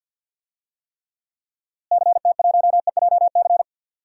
Morse code, heard as a keyed tone, about 53 wpm, 700 Hz. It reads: CT1EJC
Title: DXCC-100 Call Signs - 6 Characters - Repeated 2x - 50wpm